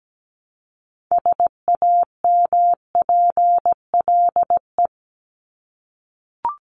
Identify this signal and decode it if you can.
Morse code, heard as a keyed tone, about 17 wpm, 700 Hz. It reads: SAMPLE
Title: Top 1501-1750 Words 17wpm